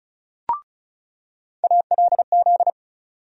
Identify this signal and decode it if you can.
Morse code, heard as a keyed tone, about 34 wpm, 700 Hz. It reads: ALZ
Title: USA Call Sign Postfixes - 3 Letters 35wpm